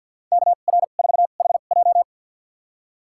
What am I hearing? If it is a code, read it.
Morse code, heard as a keyed tone, about 50 wpm, spaced as for 36 wpm, 700 Hz. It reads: KR4HJ
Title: USA Call Signs - 5 Characters - 2x Character Spacing - 50wpm